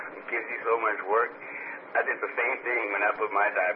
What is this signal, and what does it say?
Emergency Alert System (SAME) tones, heard as harsh buzzing bursts, with speech. It is heard from 0.2 to 2.8 s.